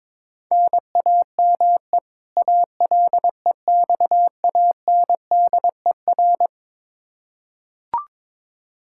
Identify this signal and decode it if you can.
Morse code, heard as a keyed tone, about 22 wpm, 700 Hz. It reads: NAME ALEXANDER
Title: Practice QSO - Exchange 1 - Beginning - 22wpm